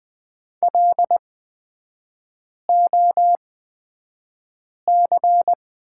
Morse code, heard as a keyed tone, about 20 wpm, spaced as for 5 wpm, 700 Hz. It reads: LOC
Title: 224 Lesson 36e - BT - Three Characters 20wpm